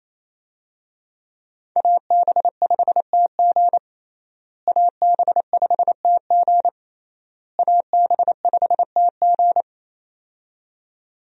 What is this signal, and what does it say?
Morse code, heard as a keyed tone, about 27 wpm, 700 Hz. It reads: AB5TG AB5TG AB5TG
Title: USA Call Signs - 5 Characters - Repeated 3x - 28wpm